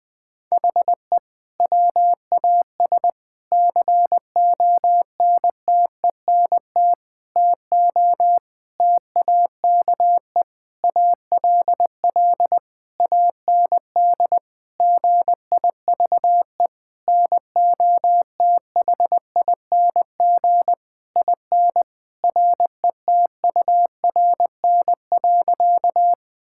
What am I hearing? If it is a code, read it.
Morse code, heard as a keyed tone, about 20 wpm, 700 Hz. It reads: HE WAS CONTENT TO TAKE ALL AND GIVE NOTHING IN RETURN.
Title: Sentences from Top 1000 Words - Review Part 6 - 20wpm